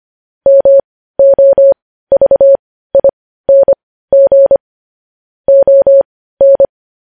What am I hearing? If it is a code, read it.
Morse code, heard as a keyed tone, about 25 wpm, spaced as for 15 wpm, 550 Hz. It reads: MOVING ON
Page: news-headlines-in-morse-code-at-15-wpm